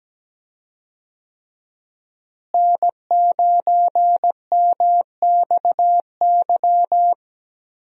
Morse code, heard as a keyed tone, about 17 wpm, 700 Hz. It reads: N9MXY
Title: USA Call Signs - 5 Characters - Repeated 3x - 17wpm